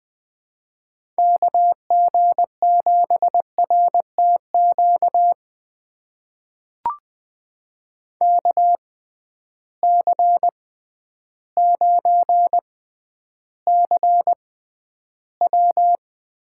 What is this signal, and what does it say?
Morse code, heard as a keyed tone, about 20 wpm, spaced as for 11 wpm, 700 Hz. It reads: KG7RTQ KC9CW
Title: USA Call Signs - 6x Character Spacing - 20wpm